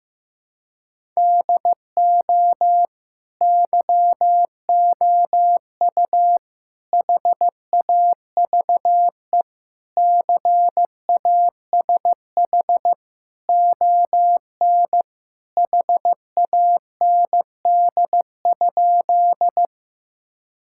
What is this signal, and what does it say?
Morse code, heard as a keyed tone, about 16 wpm, 700 Hz. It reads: DO YOU HAVE CASH ON HAND?
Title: Sentences from Top 1000 Words - Review Part 5 - 15wpm